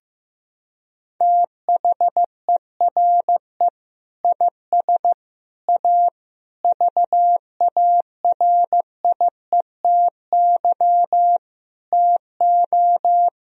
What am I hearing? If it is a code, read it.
Morse code, heard as a keyed tone, about 15 wpm, 700 Hz. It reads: THERE IS A VARIETY TO
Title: Sentences from Top 1000 Words - Review Part 4 - 15wpm